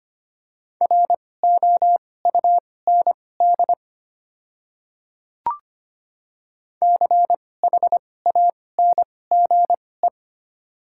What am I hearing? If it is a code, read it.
Morse code, heard as a keyed tone, about 25 wpm, spaced as for 18 wpm, 700 Hz. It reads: ROUND CHANGE
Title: 2x Character Spacing - Top 300 Words - 25wpm